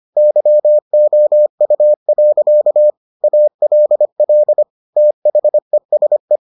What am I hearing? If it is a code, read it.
Morse code, heard as a keyed tone, about 25 wpm, 600 Hz. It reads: YOU. ALL THESE